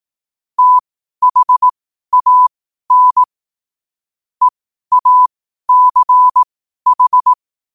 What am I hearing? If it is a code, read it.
Morse code, heard as a keyed tone, about 18 wpm, spaced as for 12 wpm, 1000 Hz. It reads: THAN EACH
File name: qotdfiles / 12wpm